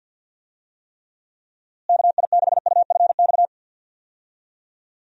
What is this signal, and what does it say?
Morse code, heard as a keyed tone, about 52 wpm, 700 Hz. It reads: KI6UFX
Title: USA Call Signs - 6 Characters - Repeated 2x - 50wpm